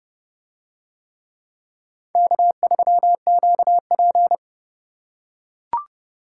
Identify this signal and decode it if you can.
Morse code, heard as a keyed tone, about 28 wpm, 700 Hz. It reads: K3QP